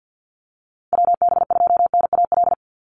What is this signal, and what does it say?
Morse code, heard as a keyed tone, about 50 wpm, 700 Hz. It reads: WB2NAL
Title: USA Call Signs 50wpm